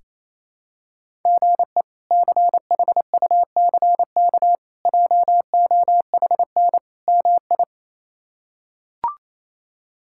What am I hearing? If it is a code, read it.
Morse code, heard as a keyed tone, about 28 wpm, 700 Hz. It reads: GE CHUCK JOHN MI
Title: K1USN SST QSO Practice - 28wpm